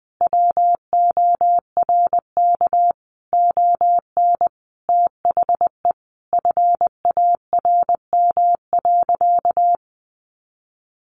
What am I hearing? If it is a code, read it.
Morse code, heard as a keyed tone, about 20 wpm, 700 Hz. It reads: WORK ON THE FARM.
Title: Sentences from Top 1200 Words 20wpm